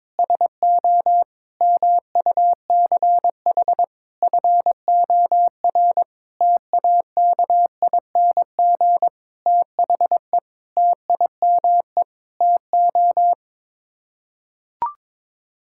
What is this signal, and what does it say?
Morse code, heard as a keyed tone, about 22 wpm, 700 Hz. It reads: SO MUCH FOR TAKING THE TIME TO
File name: Sets of 9 Words - 22wpm